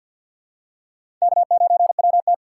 Morse code, heard as a keyed tone, about 53 wpm, 700 Hz. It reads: K9WT
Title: USA Call Signs - 3x Character Spacing - 50wpm